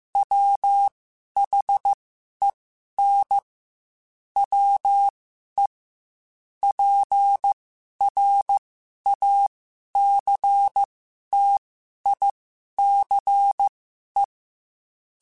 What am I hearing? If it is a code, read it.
Morse code, heard as a keyed tone, about 15 wpm, spaced as for 11 wpm, 800 Hz. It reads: WHEN WE PRACTICE